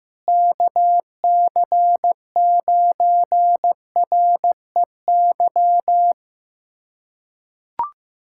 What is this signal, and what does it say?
Morse code, heard as a keyed tone, about 15 wpm, 700 Hz. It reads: KC9REY
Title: USA Callsign DE Callsign - 6 Characters - 15wpm